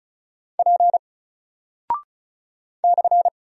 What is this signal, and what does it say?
Morse code, heard as a keyed tone, about 36 wpm, 700 Hz. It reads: P /
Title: Single Character (Letter:Number:Punctuation:Prosign) Alt 35wpm